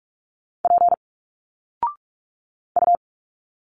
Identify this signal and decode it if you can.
Morse code, heard as a keyed tone, about 46 wpm, 700 Hz. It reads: P U
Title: Single Letters 45wpm